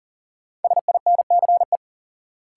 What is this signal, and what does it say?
Morse code, heard as a keyed tone, about 37 wpm, 700 Hz. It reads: SINCE
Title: Sentences from Top 300 Words - New Word Drill - 40wpm